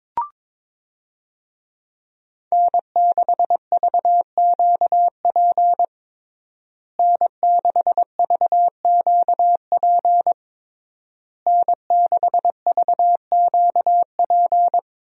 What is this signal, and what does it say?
Morse code, heard as a keyed tone, about 22 wpm, 700 Hz. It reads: N6VQP N6VQP N6VQP
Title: USA Call Signs - 5 Characters - Repeated 3x - 22wpm